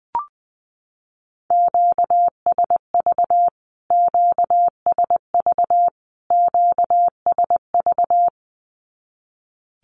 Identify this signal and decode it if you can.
Morse code, heard as a keyed tone, about 20 wpm, 700 Hz. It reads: QSV QSV QSV
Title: Q-Codes Repeated 3x 20wpm